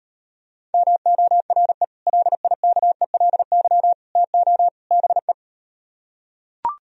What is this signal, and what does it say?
Morse code, heard as a keyed tone, about 38 wpm, 700 Hz. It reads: MORE LIKELY TO BE